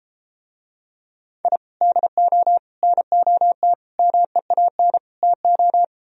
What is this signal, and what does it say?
Morse code, heard as a keyed tone, about 33 wpm, 700 Hz. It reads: I DO NOT MEAN TO